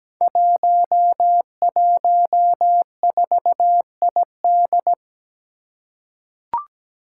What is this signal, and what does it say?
Morse code, heard as a keyed tone, about 17 wpm, 700 Hz. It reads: 114ID
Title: Five Challenging Letter-Number Combinations 17wpm